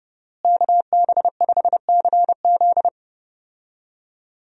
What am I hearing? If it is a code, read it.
Morse code, heard as a keyed tone, about 32 wpm, 700 Hz. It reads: KB5CZ
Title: USA Call Signs - 5 Characters - Repeated 2x - 30wpm